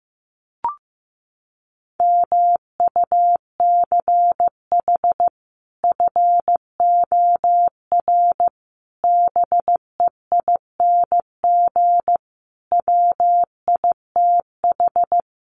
Morse code, heard as a keyed tone, about 15 wpm, 700 Hz. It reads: MUCH FOR BEING WITH
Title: Sets of 5 Words 15wpm